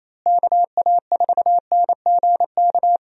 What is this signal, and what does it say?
Morse code, heard as a keyed tone, about 28 wpm, 700 Hz. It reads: KA4NGK
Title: USA Call Signs - 6 Characters - Repeated 3x - 28wpm